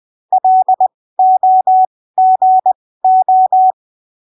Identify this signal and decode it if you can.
Morse code, heard as a keyed tone, about 20 wpm, spaced as for 15 wpm, 750 Hz. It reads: LOGO
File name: arquivos / texto / texto 08 20ppm